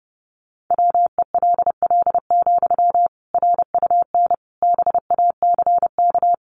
Morse code, heard as a keyed tone, about 30 wpm, 700 Hz. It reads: WELL, RUN BACK
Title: Sentences from Top 300 Words 30wpm